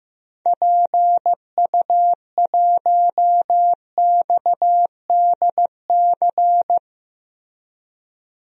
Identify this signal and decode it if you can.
Morse code, heard as a keyed tone, about 15 wpm, 700 Hz. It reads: PU1XDC
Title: DXCC-100 Call Signs - 6 Characters - Repeated 2x - 15wpm